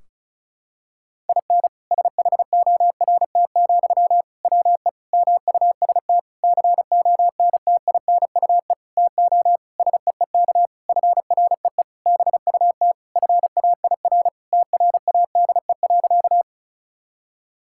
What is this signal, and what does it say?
Morse code, heard as a keyed tone, about 36 wpm, 700 Hz. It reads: IN SHORT, WE MUST CONTINUE TO SEEK FREE BUT FAIR TRADE.
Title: Sentences from Top 1600 Words 35wpm